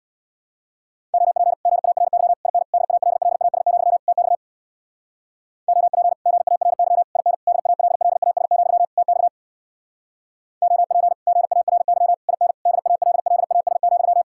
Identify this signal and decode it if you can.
Morse code, heard as a keyed tone, about 63 wpm, 700 Hz. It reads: OP GARY IN DALLAS, TX OP GARY IN DALLAS, TX OP GARY IN DALLAS,